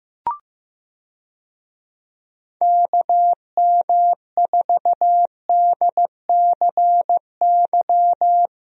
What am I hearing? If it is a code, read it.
Morse code, heard as a keyed tone, about 15 wpm, 700 Hz. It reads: KM4DCY
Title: USA Call Signs - 6 Characters - Repeated 2x - 15wpm